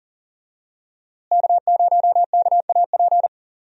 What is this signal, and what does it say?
Morse code, heard as a keyed tone, about 40 wpm, 700 Hz. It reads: K0KAP
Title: USA Call Signs - 5 Characters - Repeated 3x - 40wpm